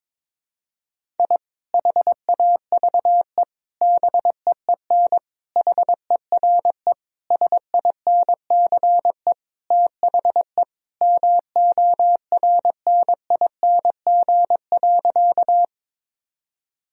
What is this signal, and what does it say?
Morse code, heard as a keyed tone, about 22 wpm, 700 Hz. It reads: I HAVE BEEN HERE SINCE THE MORNING.
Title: Sentences from Top 1000 Words - Review Part 1 - 22wpm